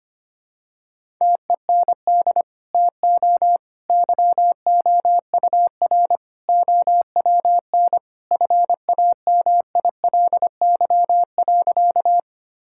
Morse code, heard as a keyed tone, about 25 wpm, 700 Hz. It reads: TEND TO YOUR OWN FAMILY.
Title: Sentences from Top 1000 Words - Review Part 5 - 25wpm